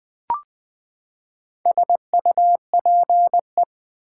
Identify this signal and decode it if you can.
Morse code, heard as a keyed tone, about 20 wpm, 700 Hz. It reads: SUPE
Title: Build A Word - 4 Letters At A Time - 11-Letter Words - 20wpm